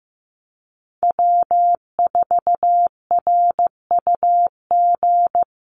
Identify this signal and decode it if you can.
Morse code, heard as a keyed tone, about 15 wpm, 700 Hz. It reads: W4RUG
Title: USA Call Signs Repeated 2x 15wpm